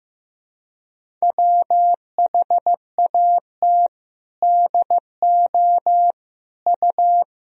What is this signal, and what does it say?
Morse code, heard as a keyed tone, about 16 wpm, 700 Hz. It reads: WHAT DO U
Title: Sentences from Top 100 Words and Abbreviations 15wpm